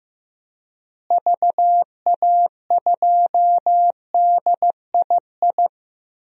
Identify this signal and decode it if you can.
Morse code, heard as a keyed tone, about 15 wpm, 700 Hz. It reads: VA2DII
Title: DXCC-100 Call Signs - 6 Characters - Repeated 2x - 15wpm